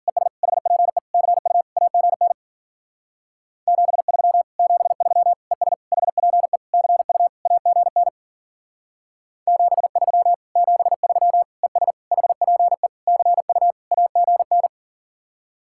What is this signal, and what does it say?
Morse code, heard as a keyed tone, about 51 wpm, 700 Hz. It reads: ES HPE CU AGN 73 73 ES HPE CU AGN 73 73 ES HPE CU AGN